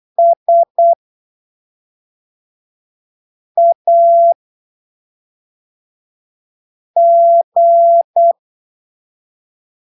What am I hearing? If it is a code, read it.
Morse code, heard as a keyed tone, about 8 wpm, 680 Hz. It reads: S A G
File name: arquivos / texto / texto 04 08ppm